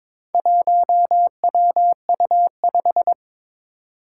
Morse code, heard as a keyed tone, about 22 wpm, 700 Hz. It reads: 1WU5
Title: Four Challenging Letter-Number Combinations 22wpm